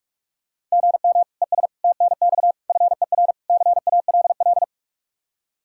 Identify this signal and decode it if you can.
Morse code, heard as a keyed tone, about 46 wpm, 700 Hz. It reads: GM ES TNX FER CALL